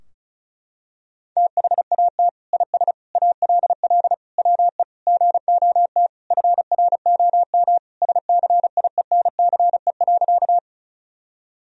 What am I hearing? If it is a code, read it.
Morse code, heard as a keyed tone, about 36 wpm, 700 Hz. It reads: THAT IS ALL WE GOT FROM SCIENCE.
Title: Sentences from Top 1000 Words - Review Part 2 - 35wpm